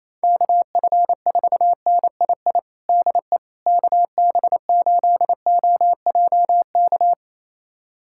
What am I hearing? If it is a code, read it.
Morse code, heard as a keyed tone, about 28 wpm, 700 Hz. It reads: KF4NII DE KB8OJK
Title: USA Callsign DE Callsign - 6 Characters - 28wpm